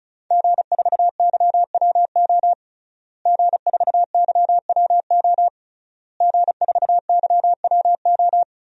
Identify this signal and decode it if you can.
Morse code, heard as a keyed tone, about 34 wpm, 700 Hz. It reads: G4YWO G4YWO G4YWO
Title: DXCC-100 Call Signs - 5 Characters - Repeated 3x - 35wpm